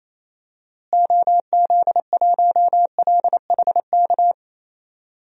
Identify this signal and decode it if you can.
Morse code, heard as a keyed tone, about 28 wpm, 700 Hz. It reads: OZ1LHK
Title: DXCC-100 Call Signs - 6 Characters - Repeated 2x - 28wpm